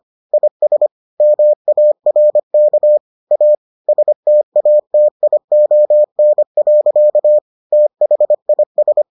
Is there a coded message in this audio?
Yes: Morse code from 0.3 to 9.0 s, heard as a keyed tone, about 25 wpm, 600 Hz. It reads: IS MARK A STATION. THIS